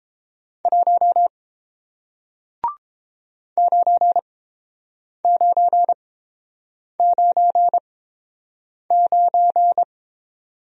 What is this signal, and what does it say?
Morse code, heard as a keyed tone, about 30 wpm, spaced as for 7 wpm, 700 Hz. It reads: 1 9999